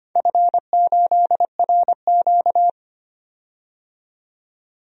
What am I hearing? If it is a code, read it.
Morse code, heard as a keyed tone, about 25 wpm, 700 Hz. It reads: F8RQ
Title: DXCC-100 Call Signs - 4 Characters - 25wpm